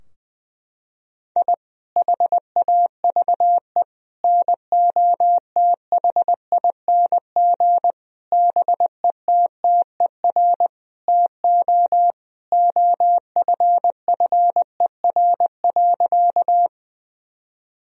Morse code, heard as a keyed tone, about 20 wpm, 700 Hz. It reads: I HAVE NOTHING BETTER TO OFFER.
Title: Sentences from Top 1000 Words - Review Part 4 - 20wpm